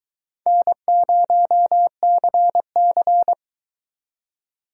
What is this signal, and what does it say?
Morse code, heard as a keyed tone, about 23 wpm, 700 Hz. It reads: N0CC